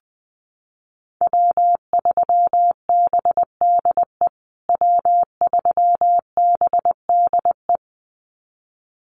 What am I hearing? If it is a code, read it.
Morse code, heard as a keyed tone, about 19 wpm, 700 Hz. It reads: W3BDE W3BDE
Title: USA Call Signs Repeated 2x 20wpm